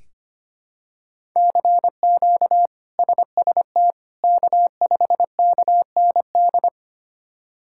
Morse code, heard as a keyed tone, about 25 wpm, 700 Hz. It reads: CQ SST K5KND
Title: K1USN SST QSO Practice - 25wpm